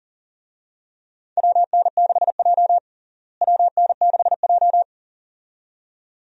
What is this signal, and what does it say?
Morse code, heard as a keyed tone, about 40 wpm, 700 Hz. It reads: WN6J WN6J
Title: USA Call Signs - 4 Characters - Repeated 2x - 40wpm